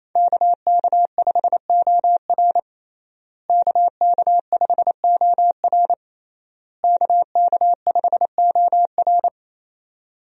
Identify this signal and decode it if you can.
Morse code, heard as a keyed tone, about 28 wpm, 700 Hz. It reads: KK5OR KK5OR KK5OR
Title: USA Call Signs - 5 Characters - Repeated 3x - 28wpm